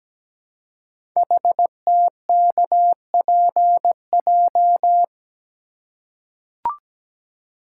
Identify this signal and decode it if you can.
Morse code, heard as a keyed tone, about 17 wpm, 700 Hz. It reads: HTKPJ
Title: Code Groups - 5 Letter:Numbers - 17wpm